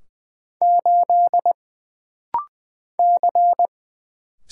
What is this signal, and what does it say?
Morse code, heard as a keyed tone, about 20 wpm, 700 Hz. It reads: 8 C
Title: Single Letter-Number - Rapid-Fire 20wpm